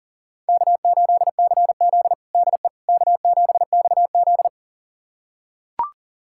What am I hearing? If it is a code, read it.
Morse code, heard as a keyed tone, about 37 wpm, 700 Hz. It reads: K8CZ DE K7XZ
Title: USA Callsign DE Callsign - 4 Characters - 40wpm